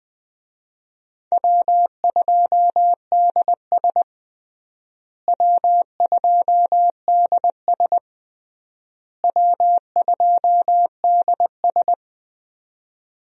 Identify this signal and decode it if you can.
Morse code, heard as a keyed tone, about 20 wpm, 700 Hz. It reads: W2DS W2DS W2DS
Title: USA Call Signs - 4 Characters - Repeated 3x - 20wpm